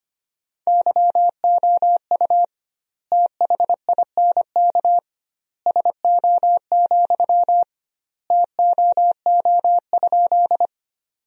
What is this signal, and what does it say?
Morse code, heard as a keyed tone, about 25 wpm, 700 Hz. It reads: YOU THINK SO, TOO?
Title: 2x Word-Spacing - Sentences from Top 200 Words - 25wpm